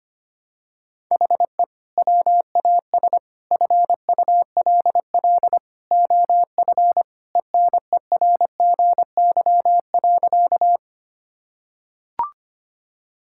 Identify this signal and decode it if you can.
Morse code, heard as a keyed tone, about 25 wpm, 700 Hz. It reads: HE WAS FULL OF ENERGY.
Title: Sentences from Top 1000 Words - Review Part 6 - 25wpm